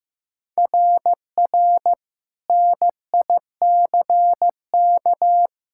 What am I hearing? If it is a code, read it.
Morse code, heard as a keyed tone, about 15 wpm, 700 Hz. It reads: RR NICK
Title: SOTA QSO Practice 15wpm